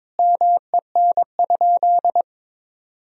Morse code, heard as a keyed tone, about 21 wpm, 700 Hz. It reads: MEN?
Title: Sentences from Top 1000 Words - Review Part 1 - 22wpm